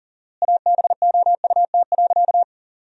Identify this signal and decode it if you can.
Morse code, heard as a keyed tone, about 40 wpm, 700 Hz. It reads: ABOUT.
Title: 4x Word-Spacing - Sentences from Top 400 Words - 40wpm